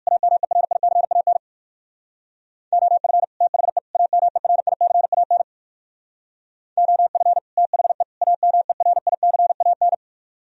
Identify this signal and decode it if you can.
Morse code, heard as a keyed tone, about 61 wpm, 700 Hz. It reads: AMERICAN OF THE AMERICAN OF THE AMERICAN